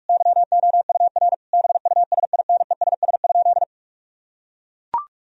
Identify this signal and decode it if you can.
Morse code, heard as a keyed tone, about 41 wpm, 700 Hz. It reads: YOUR BUSINESS?
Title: Sentences from Top 1000 Words - Review Part 4 - 45wpm